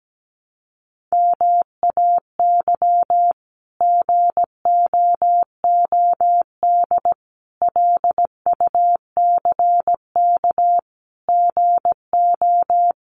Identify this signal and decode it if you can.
Morse code, heard as a keyed tone, about 17 wpm, 700 Hz. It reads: MAY GOOD LUCK GO
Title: Sentences from Top 1500 Words 17wpm